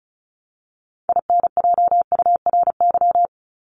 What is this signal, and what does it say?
Morse code, heard as a keyed tone, about 34 wpm, 700 Hz. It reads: INJURY
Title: Top 1751-2000 Words 35wpm